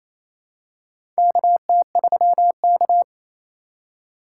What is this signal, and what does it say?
Morse code, heard as a keyed tone, about 28 wpm, 700 Hz. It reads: KT3K
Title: USA Call Signs - 4 Characters - 2x Character Spacing - 28wpm